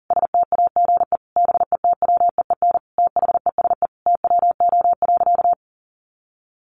Morse code, heard as a keyed tone, about 40 wpm, 700 Hz. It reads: STAGE BETWEEN THESE TWO.
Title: Sentences from Top 1200 Words 40wpm